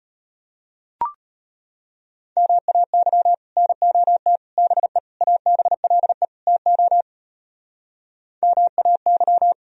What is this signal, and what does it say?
Morse code, heard as a keyed tone, about 32 wpm, 700 Hz. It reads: MAY NOT BE ABLE TO MAY